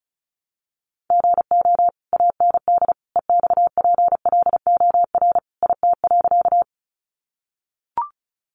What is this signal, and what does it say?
Morse code, heard as a keyed tone, about 34 wpm, 700 Hz. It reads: GO AND EXPLOR IT.
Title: Sentences from Top 1100 Words 35wpm